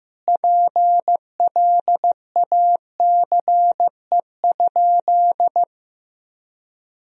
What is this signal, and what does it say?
Morse code, heard as a keyed tone, about 16 wpm, 700 Hz. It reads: PLACE?
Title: Sentences from Top 200 Words and Abbreviations 15wpm